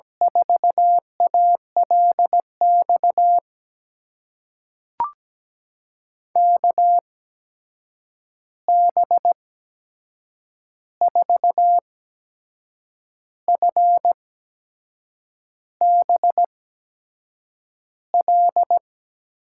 Morse code, heard as a keyed tone, about 17 wpm, spaced as for 6 wpm, 700 Hz. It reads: 4ALX KB4FBL
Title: USA Call Signs - 6 Characters - 8x Character Spacing - 17wpm